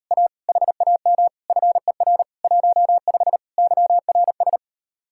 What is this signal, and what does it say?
Morse code, heard as a keyed tone, about 38 wpm, 700 Hz. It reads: A HAM FER 15 YRS